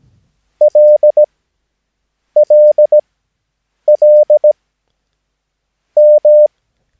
Morse code, heard as a keyed tone, about 17 wpm, spaced as for 6 wpm, 600 Hz. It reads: LLLM
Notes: Övningsmaterial med alfabetet A till Ö samt siffror 0 till 9 enligt International/Continental Morse Code.